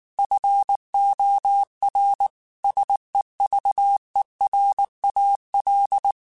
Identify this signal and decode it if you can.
Morse code, heard as a keyed tone, about 19 wpm, 800 Hz. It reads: FOR SEVERAL
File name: QSOs / QSOs-mp3s / WA7QOX 2@18